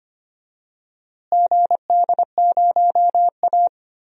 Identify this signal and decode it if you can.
Morse code, heard as a keyed tone, about 25 wpm, 700 Hz. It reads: GD0A
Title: DXCC-100 Call Signs - 4 Characters - Repeated 2x - 25wpm